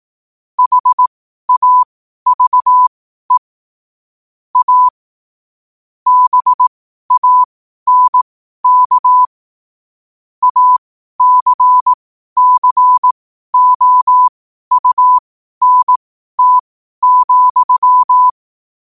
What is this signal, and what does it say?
Morse code, heard as a keyed tone, about 18 wpm, spaced as for 12 wpm, 1000 Hz. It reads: HAVE A BANK ACCOUNT,
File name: qotdfiles / 12wpm